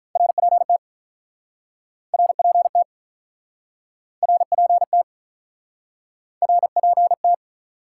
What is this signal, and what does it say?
Morse code, heard as a keyed tone, about 48 wpm, 700 Hz. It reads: RPT RPT RPT RPT